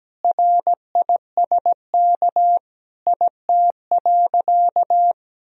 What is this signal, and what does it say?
Morse code, heard as a keyed tone, about 17 wpm, 700 Hz. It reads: RISK IT.
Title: Sentences from Top 1000 Words - Review Part 4 - 17wpm